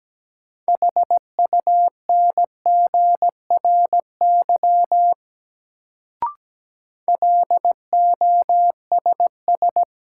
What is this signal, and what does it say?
Morse code, heard as a keyed tone, about 17 wpm, 700 Hz. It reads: HUNGRY LOSS
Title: Sentences from Top 1400 Words - New Word Drill - 17wpm